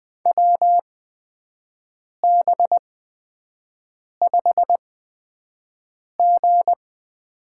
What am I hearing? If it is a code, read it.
Morse code, heard as a keyed tone, about 20 wpm, spaced as for 5 wpm, 700 Hz. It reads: WB5G
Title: USA Call Signs - 6 Characters - 8x Character Spacing - 20wpm